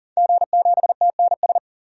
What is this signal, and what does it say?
Morse code, heard as a keyed tone, about 37 wpm, 700 Hz. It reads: G7TNS
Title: DXCC-100 Call Signs - 5 Characters - 40wpm